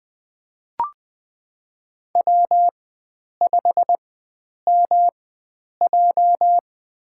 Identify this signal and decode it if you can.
Morse code, heard as a keyed tone, about 20 wpm, spaced as for 9 wpm, 700 Hz. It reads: W5MJ
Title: USA Call Signs - 4 Characters - 4x Character Spacing - 20wpm